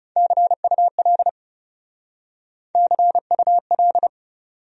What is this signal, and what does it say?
Morse code, heard as a keyed tone, about 39 wpm, 700 Hz. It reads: CUL CUL